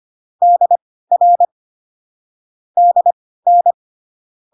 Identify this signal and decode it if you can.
Morse code, heard as a keyed tone, about 25 wpm, spaced as for 16 wpm, 700 Hz. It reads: DR DN